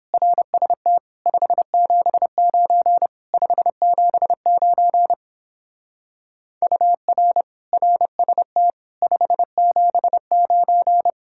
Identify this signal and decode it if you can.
Morse code, heard as a keyed tone, about 31 wpm, 700 Hz. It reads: RST 579 579 UR RST 579